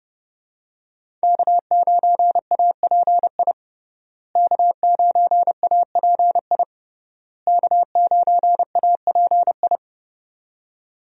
Morse code, heard as a keyed tone, about 30 wpm, 700 Hz. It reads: K9API K9API K9API
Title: USA Call Signs - 5 Characters - Repeated 3x - 30wpm